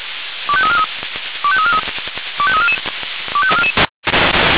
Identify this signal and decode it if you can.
Five-tone selective call (ZVEI), heard as quick runs of five tones, tones of about 80 ms. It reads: 25343 25343 25344
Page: /496/SDR/ZVEI/